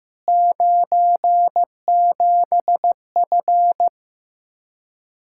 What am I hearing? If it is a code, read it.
Morse code, heard as a keyed tone, about 15 wpm, 700 Hz. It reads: 97F
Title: Speed-Racing Common QSO Phrases Encore - 15wpm